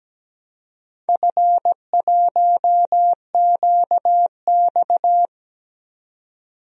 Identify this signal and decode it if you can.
Morse code, heard as a keyed tone, about 17 wpm, 700 Hz. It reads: F1QX
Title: DXCC-100 Call Signs - 4 Characters - 17wpm